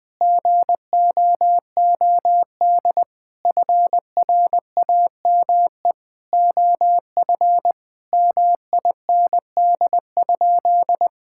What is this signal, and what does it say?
Morse code, heard as a keyed tone, about 20 wpm, 700 Hz. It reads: GOOD FRAME OF MIND?
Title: Sentences from Top 1000 Words - Review Part 6 - 20wpm